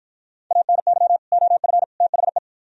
Morse code, heard as a keyed tone, about 49 wpm, 700 Hz. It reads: ANY OF THE